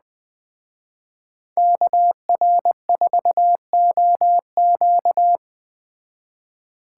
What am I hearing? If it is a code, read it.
Morse code, heard as a keyed tone, about 20 wpm, 700 Hz. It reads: KR4OQ
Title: Speed Racing - USA 2x2 Call Signs 17wpm